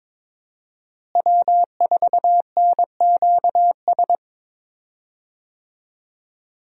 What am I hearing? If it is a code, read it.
Morse code, heard as a keyed tone, about 21 wpm, 700 Hz. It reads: W4NQS
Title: USA Call Signs - 5 Characters - Repeated 2x - 22wpm